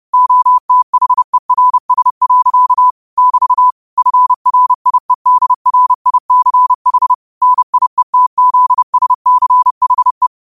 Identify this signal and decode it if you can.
Morse code, heard as a keyed tone, about 30 wpm, 1000 Hz. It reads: OTHERS. <BT> FRIEDRICH NIETZSCHE
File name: qotdfiles / 30wpm